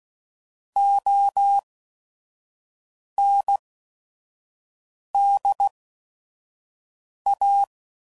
Morse code, heard as a keyed tone, about 16 wpm, spaced as for 5 wpm, 800 Hz. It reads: OND A
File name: CW Code Groups @5WPM